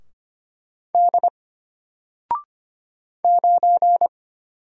Morse code, heard as a keyed tone, about 25 wpm, 700 Hz. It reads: D 9
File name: Single-Letter-Number-25wpm